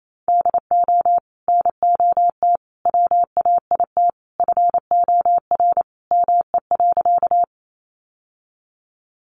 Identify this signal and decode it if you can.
Morse code, heard as a keyed tone, about 28 wpm, 700 Hz. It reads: DO NOT WAIT FOR ME.
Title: Sentences from Top 600 Words 28wpm